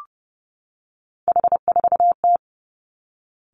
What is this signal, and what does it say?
Morse code, heard as a keyed tone, about 30 wpm, 700 Hz. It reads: H4T
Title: Three Challenging Letter-Number Combinations 30wpm